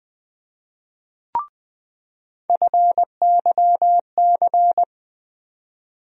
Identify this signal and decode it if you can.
Morse code, heard as a keyed tone, about 20 wpm, 700 Hz. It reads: FYC
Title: USA Call Sign Postfixes - 3 Letters 20wpm